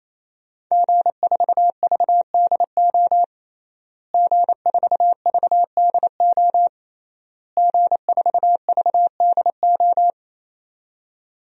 Morse code, heard as a keyed tone, about 28 wpm, 700 Hz. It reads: G4VDO G4VDO G4VDO
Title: DXCC-100 Call Signs - 5 Characters - Repeated 3x - 28wpm